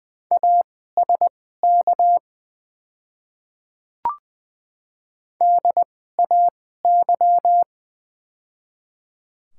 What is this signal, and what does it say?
Morse code, heard as a keyed tone, about 20 wpm, spaced as for 14 wpm, 700 Hz. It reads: ASK DAY
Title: 2x Character Spacing - Top 300 Words - 20wpm